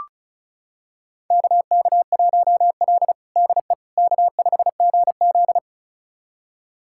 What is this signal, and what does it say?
Morse code, heard as a keyed tone, about 36 wpm, 700 Hz. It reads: KK1L DE K5GZ
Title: USA Callsign DE Callsign - 4 Characters - 35wpm